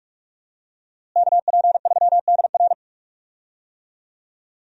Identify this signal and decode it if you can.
Morse code, heard as a keyed tone, about 46 wpm, 700 Hz. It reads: KP3DR
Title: USA Call Signs - 5 Characters - Repeated 2x - 45wpm